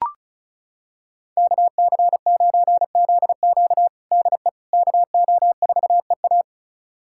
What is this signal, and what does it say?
Morse code, heard as a keyed tone, about 36 wpm, 700 Hz. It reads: KC9ZQ DE KO4EA
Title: USA Callsign DE Callsign - 5 Characters - 35wpm